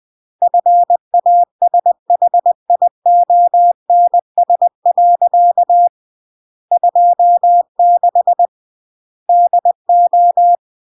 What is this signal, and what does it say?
Morse code, heard as a keyed tone, about 20 wpm, 700 Hz. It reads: FASHIONS. 26 DO
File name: CW / jodorowsky / 82 commandments 20-20